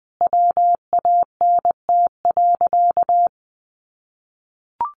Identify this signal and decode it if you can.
Morse code, heard as a keyed tone, about 20 wpm, 700 Hz. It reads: WANT.
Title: Sentences from Top 1100 Words 20wpm